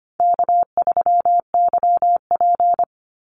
Morse code, heard as a keyed tone, about 25 wpm, 700 Hz. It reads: K3YP
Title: Easy Call Signs - 25wpm